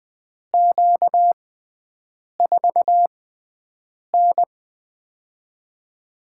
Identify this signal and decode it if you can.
Morse code, heard as a keyed tone, about 20 wpm, spaced as for 7 wpm, 700 Hz. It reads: Q4N
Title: USA Call Signs - 4 Characters - 6x Character Spacing - 20wpm